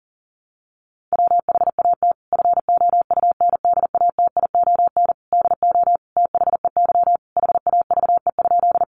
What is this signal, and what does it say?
Morse code, heard as a keyed tone, about 40 wpm, 700 Hz. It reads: WHAT FOUNDATION DO THEY HAVE?
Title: Sentences from Top 1200 Words 40wpm